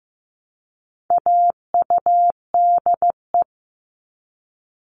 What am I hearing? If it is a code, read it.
Morse code, heard as a keyed tone, about 15 wpm, 700 Hz. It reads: AUDE
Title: All Cut Numbers - 4-Digit 15wpm